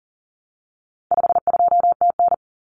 Morse code, heard as a keyed tone, about 40 wpm, 700 Hz. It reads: 52TN
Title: Common Cut Numbers - 4-Digit 40wpm